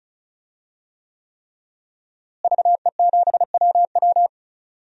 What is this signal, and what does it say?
Morse code, heard as a keyed tone, about 36 wpm, 700 Hz. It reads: VE7WW
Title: DXCC-100 Call Signs - 5 Characters - Repeated 3x - 35wpm